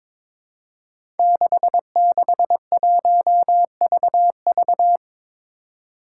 Morse code, heard as a keyed tone, about 22 wpm, 700 Hz. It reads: 661VV
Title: Five Challenging Letter-Number Combinations 22wpm